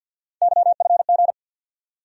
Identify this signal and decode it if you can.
Morse code, heard as a keyed tone, about 54 wpm, 700 Hz. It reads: YFG
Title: USA Call Sign Postfixes - 3 Letters 50wpm